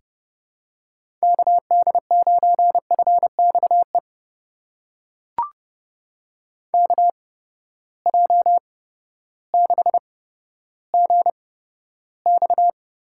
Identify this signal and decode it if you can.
Morse code, heard as a keyed tone, about 30 wpm, spaced as for 14 wpm, 700 Hz. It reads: KD9FXE KJ6GX
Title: USA Call Signs - 6 Characters - 8x Character Spacing - 30wpm